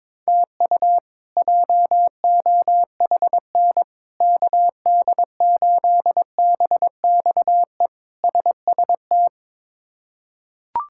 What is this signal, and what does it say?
Morse code, heard as a keyed tone, about 22 wpm, 700 Hz. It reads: TU JOHN KD8BXE SST
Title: K1USN SST QSO Practice - 22wpm